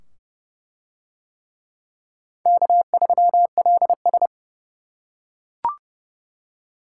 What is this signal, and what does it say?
Morse code, heard as a keyed tone, about 28 wpm, 700 Hz. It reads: K3LS